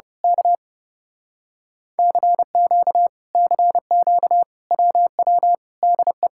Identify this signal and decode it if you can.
Morse code, heard as a keyed tone, about 29 wpm, 700 Hz. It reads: K CQ CQ WW DE
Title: Speed-Racing Common QSO Phrases - 30wpm